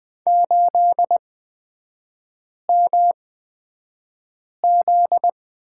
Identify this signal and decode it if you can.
Morse code, heard as a keyed tone, about 20 wpm, spaced as for 5 wpm, 700 Hz. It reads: 8MZ
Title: 243 Lesson 39e - Letter z - Three Characters 20wpm